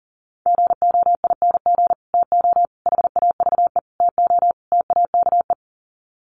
Morse code, heard as a keyed tone, about 40 wpm, 700 Hz. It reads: GOING TO HAVE TO TAKE
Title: Sets of 5 Words Encore 40wpm